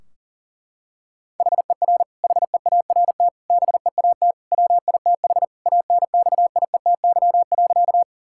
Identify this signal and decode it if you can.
Morse code, heard as a keyed tone, about 40 wpm, 700 Hz. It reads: HER HEART BEAT WITH ANXIETY.
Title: Sentences from Top 1600 Words 40wpm